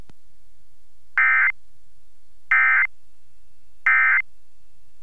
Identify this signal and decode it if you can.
Emergency Alert System (SAME) tones, heard as harsh buzzing bursts. It is heard from 1.2 to 4.2 s.